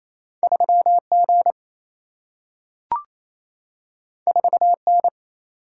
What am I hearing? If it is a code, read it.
Morse code, heard as a keyed tone, about 28 wpm, 700 Hz. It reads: 3G 4N
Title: Single-Letter Single-Number 28wpm